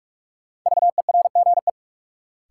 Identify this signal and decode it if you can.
Morse code, heard as a keyed tone, about 47 wpm, 700 Hz. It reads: VERGE
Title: Sets of 4 Words - New Word Drill - 45wpm